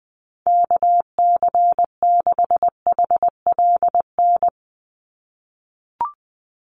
Morse code, heard as a keyed tone, about 20 wpm, 700 Hz. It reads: KC6HLN
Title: USA Call Signs 20wpm